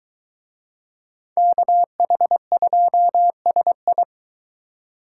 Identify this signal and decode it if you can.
Morse code, heard as a keyed tone, about 22 wpm, 700 Hz. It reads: KH2SI